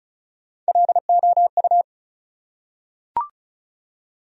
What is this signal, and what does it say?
Morse code, heard as a keyed tone, about 35 wpm, 700 Hz. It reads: LOU
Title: USA Common Names - Max 4 Letters - 35wpm